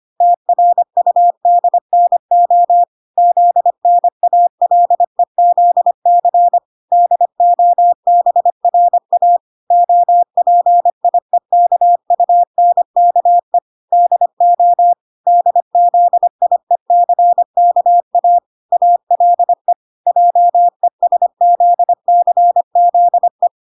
Morse code, heard as a keyed tone, about 25 wpm, 700 Hz. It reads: TRUDNO ZNALEZC DOBRA OPIEKUNKE DO DZIECKA ALE JESZCZE
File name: 25wpm